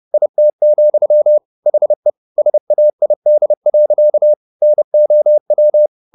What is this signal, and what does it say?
Morse code, heard as a keyed tone, about 30 wpm, 600 Hz. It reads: IT, HE SAID. NOW
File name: the old man and the sea / the old man and the sea 1 30wpm